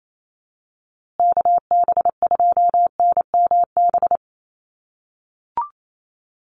Morse code, heard as a keyed tone, about 28 wpm, 700 Hz. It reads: KB2NMB
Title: USA Call Signs Repeated 2x 28wpm